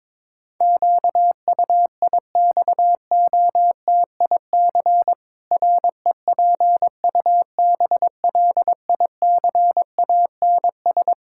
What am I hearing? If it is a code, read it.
Morse code, heard as a keyed tone, about 21 wpm, 700 Hz. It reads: QUIXOTIC REPUBLICANS
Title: Pangram Challenge v2 - 22wpm